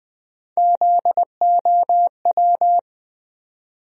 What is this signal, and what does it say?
Morse code, heard as a keyed tone, about 20 wpm, 700 Hz. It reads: ZOW
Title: USA Call Sign Postfixes - 3 Letters 20wpm